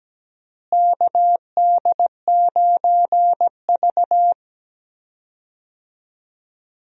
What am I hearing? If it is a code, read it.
Morse code, heard as a keyed tone, about 17 wpm, 700 Hz. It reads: KD9V
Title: USA Call Signs - 4 Characters - Repeated 2x - 17wpm